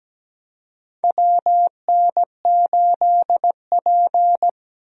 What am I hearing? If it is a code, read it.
Morse code, heard as a keyed tone, about 17 wpm, 700 Hz. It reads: WN8P
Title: USA Call Signs - 4 Characters - Repeated 3x - 17wpm